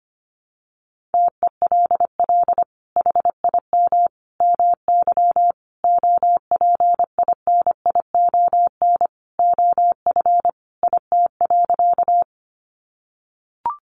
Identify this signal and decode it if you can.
Morse code, heard as a keyed tone, about 25 wpm, 700 Hz. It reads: TELL HIM MY OPINION OF IT.
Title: Sentences from Top 1200 Words 25wpm